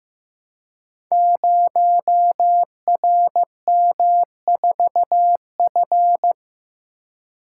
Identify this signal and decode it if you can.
Morse code, heard as a keyed tone, about 15 wpm, 700 Hz. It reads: 0RM4F
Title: Code Groups - 5 Letter:Numbers - 15wpm